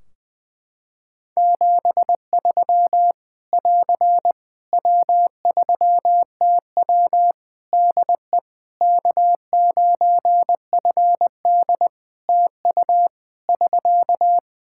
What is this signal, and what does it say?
Morse code, heard as a keyed tone, about 21 wpm, 700 Hz. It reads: 73 <AR> W3TW DE K9FD TU <SK>
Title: Practice QSO - Exchange 4 - Ending - 20wpm